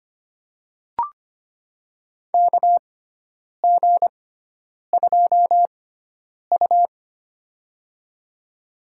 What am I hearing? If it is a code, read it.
Morse code, heard as a keyed tone, about 25 wpm, spaced as for 9 wpm, 700 Hz. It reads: KG2U
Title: USA Call Signs - 4 Characters - 6x Character Spacing - 25wpm